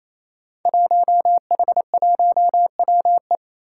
Morse code, heard as a keyed tone, about 28 wpm, 700 Hz. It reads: 1H1WE
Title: Five Challenging Letter-Number Combinations 28wpm